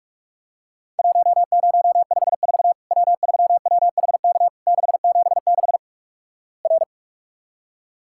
Morse code, heard as a keyed tone, about 46 wpm, 700 Hz. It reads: 1054 W3WHK 676 R
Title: Worked All Europe DX Contest - QTC Exchange - 45wpm